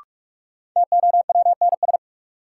Morse code, heard as a keyed tone, about 46 wpm, 700 Hz. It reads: TOWNS
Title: Sentences from Top 1000 Words - Review Part 6 - New Word Drill - 45wpm